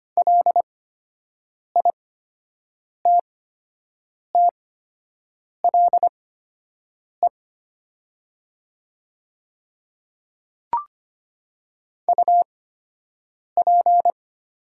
Morse code, heard as a keyed tone, about 25 wpm, spaced as for 7 wpm, 700 Hz. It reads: LITTLE UP
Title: 8x Character Spacing - Top 200 Words - 25wpm